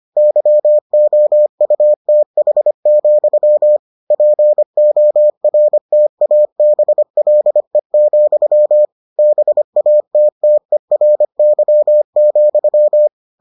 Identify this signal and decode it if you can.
Morse code, heard as a keyed tone, about 25 wpm, 600 Hz. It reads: YOUTH, PORTABLE, BATTERY,